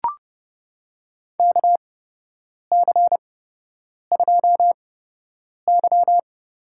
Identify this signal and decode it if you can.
Morse code, heard as a keyed tone, about 30 wpm, spaced as for 8 wpm, 700 Hz. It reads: KC2Y
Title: USA Call Signs - 6 Characters - 8x Character Spacing - 30wpm